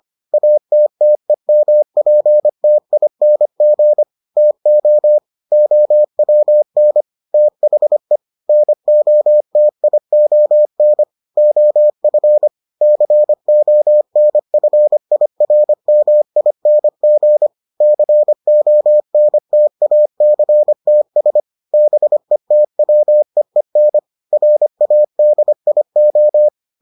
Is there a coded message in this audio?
Yes: Morse code from 0.3 to 26.5 s, heard as a keyed tone, about 25 wpm, 600 Hz. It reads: ATTEMPTING TO OWN THE NOTION OF CONFIRMING CONTACTS BETWEEN RADIO